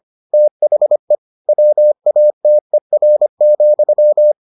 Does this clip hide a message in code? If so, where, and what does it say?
Yes: Morse code from 0.3 to 4.3 s, heard as a keyed tone, about 25 wpm, 600 Hz. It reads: THE WATER,